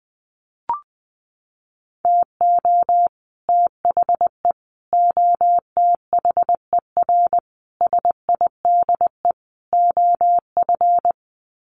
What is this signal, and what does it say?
Morse code, heard as a keyed tone, about 20 wpm, 700 Hz. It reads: TO THE OTHER SIDE OF
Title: Sets of 5 Words Encore 20wpm